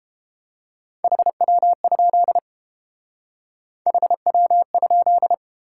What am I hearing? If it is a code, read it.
Morse code, heard as a keyed tone, about 29 wpm, 700 Hz. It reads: HW? HW?